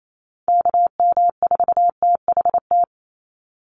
Morse code, heard as a keyed tone, about 28 wpm, 700 Hz. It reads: KM4THT
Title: USA Call Signs 28wpm